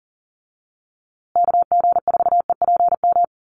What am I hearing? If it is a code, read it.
Morse code, heard as a keyed tone, about 40 wpm, 700 Hz. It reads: KG4EPM
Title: USA Call Signs Repeated 2x 40wpm